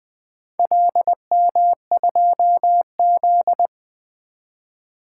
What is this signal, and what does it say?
Morse code, heard as a keyed tone, about 21 wpm, 700 Hz. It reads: LM2Z
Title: Four Letter-Number Combinations - 20wpm